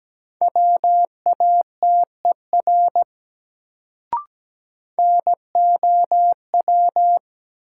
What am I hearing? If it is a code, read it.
Morse code, heard as a keyed tone, about 17 wpm, 700 Hz. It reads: WATER NOW
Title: Sentences from Top 100 Words - New Word Drill - 17wpm